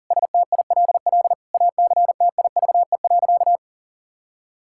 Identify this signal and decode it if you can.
Morse code, heard as a keyed tone, about 37 wpm, 700 Hz. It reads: STILL ACTIVE.
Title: Sentences from Top 1000 Words - Review Part 6 - 40wpm